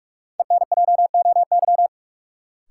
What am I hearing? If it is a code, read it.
Morse code, heard as a keyed tone, about 45 wpm, 700 Hz. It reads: ENJOY
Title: Sentences from Top 1300 Words - New Word Drill Alt - 45wpm